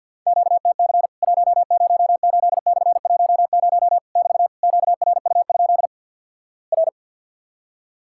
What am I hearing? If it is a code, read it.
Morse code, heard as a keyed tone, about 48 wpm, 700 Hz. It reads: QTC 108/10 <BT> QRU? R
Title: Worked All Europe DX Contest - QTC Exchange with Repeats - 50wpm